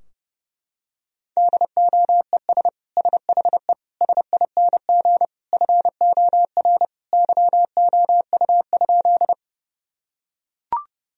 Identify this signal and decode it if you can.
Morse code, heard as a keyed tone, about 30 wpm, 700 Hz. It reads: DOES SHE SING FOR YOU?
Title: Sentences from Top 1000 Words - Review Part 2 - 30wpm